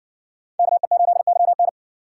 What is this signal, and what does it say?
Morse code, heard as a keyed tone, about 60 wpm, 700 Hz. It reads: KE8YN